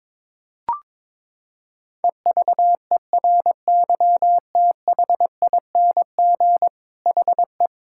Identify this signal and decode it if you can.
Morse code, heard as a keyed tone, about 22 wpm, 700 Hz. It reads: EVERYTHING HE
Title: Sentences from Top 1000 Words - Review Part 6 - 22wpm